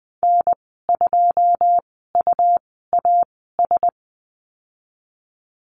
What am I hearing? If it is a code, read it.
Morse code, heard as a keyed tone, about 20 wpm, spaced as for 14 wpm, 700 Hz. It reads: N2UAS
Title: 122 Lesson 20g - Number 3 - Call Signs 20wpm